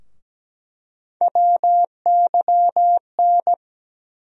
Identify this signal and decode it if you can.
Morse code, heard as a keyed tone, about 17 wpm, 700 Hz. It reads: WYN
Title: USA Call Sign Postfixes - 3 Letters 17wpm